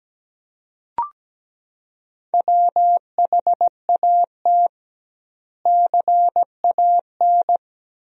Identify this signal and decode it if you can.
Morse code, heard as a keyed tone, about 17 wpm, 700 Hz. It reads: WHAT CAN
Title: 2x Word-Spacing - Sentences from Top 400 Words - 17wpm